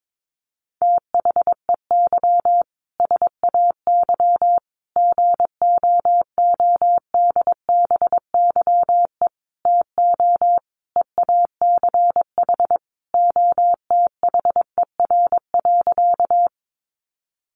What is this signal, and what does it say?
Morse code, heard as a keyed tone, about 22 wpm, 700 Hz. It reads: THEY SAY GOODBYE TO EACH OTHER.
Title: Additional Easy Phrases 22wpm